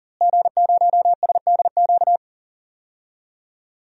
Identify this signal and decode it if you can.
Morse code, heard as a keyed tone, about 40 wpm, 700 Hz. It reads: G0SDQ
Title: DXCC-100 Call Signs - 5 Characters - Repeated 3x - 40wpm